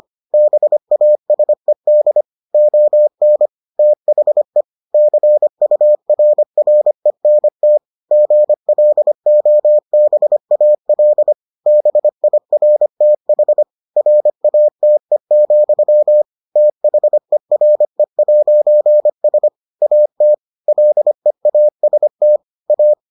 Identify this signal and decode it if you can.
Morse code, heard as a keyed tone, about 25 wpm, 600 Hz. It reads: BASED ON THE CURRENT GLOBAL BIRTH RATE, THERE'S AT LEAST A